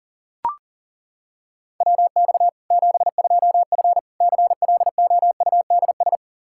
Morse code, heard as a keyed tone, about 40 wpm, 700 Hz. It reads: WX 72F CLOUDS
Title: K1USN SST QSO Practice 40wpm